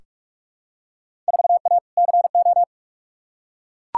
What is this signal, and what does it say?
Morse code, heard as a keyed tone, about 42 wpm, 700 Hz. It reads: 4A CO
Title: Field Day Exchanges - 45wpm